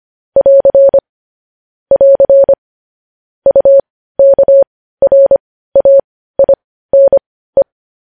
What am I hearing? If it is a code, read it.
Morse code, heard as a keyed tone, about 25 wpm, spaced as for 15 wpm, 550 Hz. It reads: <AR> <AR> UKRAINE
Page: news-headlines-in-morse-code-at-15-wpm